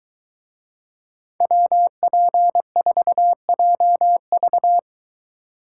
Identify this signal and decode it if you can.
Morse code, heard as a keyed tone, about 23 wpm, 700 Hz. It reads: WP4JV